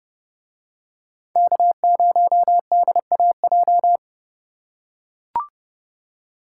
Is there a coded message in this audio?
Yes: Morse code from 1.4 to 4.0 s, heard as a keyed tone, about 29 wpm, 700 Hz. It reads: K0DAJ